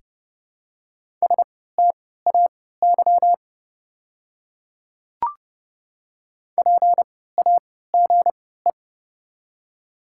Morse code, heard as a keyed tone, about 30 wpm, spaced as for 17 wpm, 700 Hz. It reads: STAY PAGE
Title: 3x Character Spacing - Top 500 Words - 30wpm